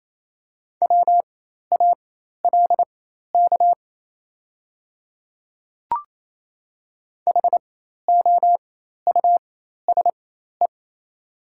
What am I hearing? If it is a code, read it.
Morse code, heard as a keyed tone, about 28 wpm, spaced as for 13 wpm, 700 Hz. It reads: WALK HOUSE
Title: 4x Character Spacing - Top 500 Words - 28wpm